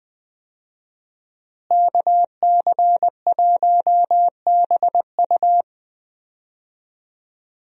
Morse code, heard as a keyed tone, about 21 wpm, 700 Hz. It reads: KC1BU
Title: USA Call Signs - 5 Characters - Repeated 3x - 20wpm